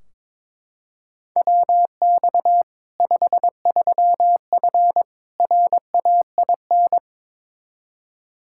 Morse code, heard as a keyed tone, about 22 wpm, 700 Hz. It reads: WX 53F RAIN
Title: Practice QSO - Exchange 2 - Info - 22wpm